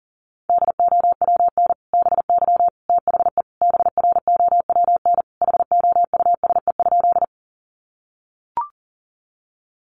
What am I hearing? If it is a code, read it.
Morse code, heard as a keyed tone, about 40 wpm, 700 Hz. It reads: DOWN BY THE BROWN HOUSE?
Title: Sentences from Top 1200 Words 40wpm